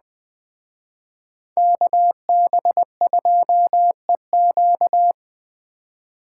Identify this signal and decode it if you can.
Morse code, heard as a keyed tone, about 20 wpm, 700 Hz. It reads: KB2EQ